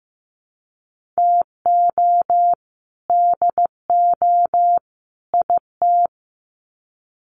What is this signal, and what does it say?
Morse code, heard as a keyed tone, about 15 wpm, 700 Hz. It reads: TO DO IT
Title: Sets of 3 Words 15wpm